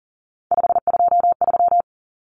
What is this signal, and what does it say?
Morse code, heard as a keyed tone, about 40 wpm, 700 Hz. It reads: 523
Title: Numbers - 3-Digit 40wpm